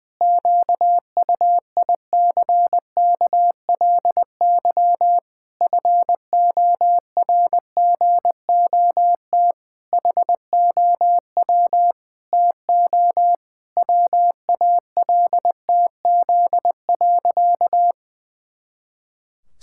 Morse code, heard as a keyed tone, about 21 wpm, 700 Hz. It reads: QUICKLY FORGOT HOW TO WALTZ.
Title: Pangram Challenge v2 - 20wpm